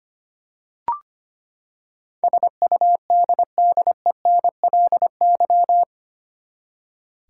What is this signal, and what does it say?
Morse code, heard as a keyed tone, about 25 wpm, 700 Hz. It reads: SUDDENLY
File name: Eight Letter Words - 25wpm